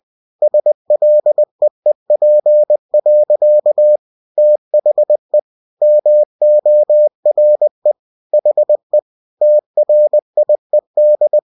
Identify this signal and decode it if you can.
Morse code, heard as a keyed tone, about 20 wpm, 600 Hz. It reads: SLEEP. THE MORE HE TRIED
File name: cw-books / winnie-the-pooh / winnie-20wpm-page-0039